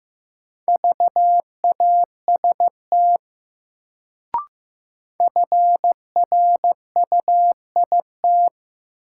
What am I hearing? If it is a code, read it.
Morse code, heard as a keyed tone, about 15 wpm, 700 Hz. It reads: VAST FRUIT
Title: Sentences from Top 1300 Words - New Word Drill Alt - 15wpm